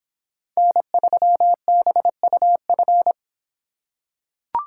Morse code, heard as a keyed tone, about 26 wpm, 700 Hz. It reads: N3BUF